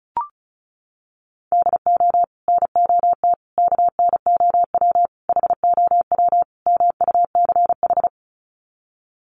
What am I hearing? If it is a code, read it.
Morse code, heard as a keyed tone, about 34 wpm, 700 Hz. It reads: DO NOT KNOW HOW MUCH
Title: Sets of 5 Words Encore 35wpm